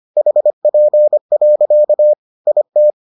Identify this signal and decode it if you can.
Morse code, heard as a keyed tone, about 25 wpm, 600 Hz. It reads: HP. IT